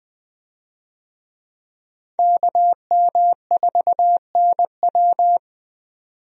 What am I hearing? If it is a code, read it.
Morse code, heard as a keyed tone, about 20 wpm, 700 Hz. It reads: KM4NW
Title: USA Call Signs - 5 Characters - 20wpm